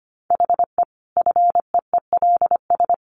Morse code, heard as a keyed tone, about 25 wpm, 700 Hz. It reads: HE FEELS
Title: Additional Easy Phrases 25wpm